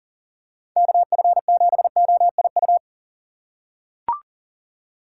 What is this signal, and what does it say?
Morse code, heard as a keyed tone, about 40 wpm, 700 Hz. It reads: KF7OIU
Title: USA Call Signs - 6 Characters - 4x Character Spacing - 40wpm